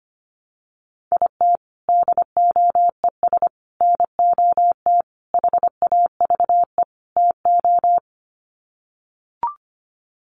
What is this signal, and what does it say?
Morse code, heard as a keyed tone, about 25 wpm, 700 Hz. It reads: IT DOES NOT HAVE TO
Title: Sets of 5 Words 25wpm